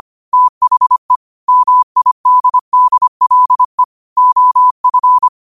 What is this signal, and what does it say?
Morse code, heard as a keyed tone, about 25 wpm, 1000 Hz. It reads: THE MIDDLE OF
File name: qotdfiles / 25wpm